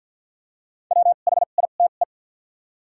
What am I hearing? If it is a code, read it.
Morse code, heard as a keyed tone, about 48 wpm, spaced as for 35 wpm, 700 Hz. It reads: WHITE
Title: 2x Character Spacing - Top 300 Words - 50wpm